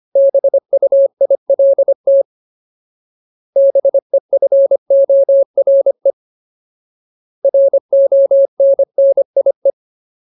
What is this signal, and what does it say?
Morse code, heard as a keyed tone, about 25 wpm, 550 Hz. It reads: BUILT BEFORE RONNIE
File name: Ultimate Word List / 25WPM Ultimate